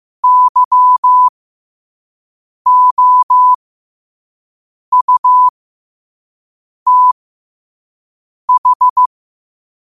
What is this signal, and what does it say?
Morse code, heard as a keyed tone, about 15 wpm, spaced as for 5 wpm, 1000 Hz. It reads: YOUTH
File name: qotdfiles / 5wpm